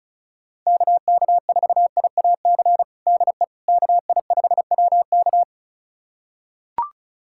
Morse code, heard as a keyed tone, about 34 wpm, 700 Hz. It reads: KK4IAC DE KI5WK
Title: USA Callsign DE Callsign 35wpm